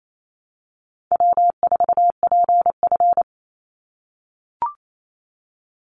Morse code, heard as a keyed tone, about 28 wpm, 700 Hz. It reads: W4PF
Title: Easy Call Signs - 28wpm